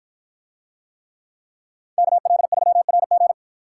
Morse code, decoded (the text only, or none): KL3RG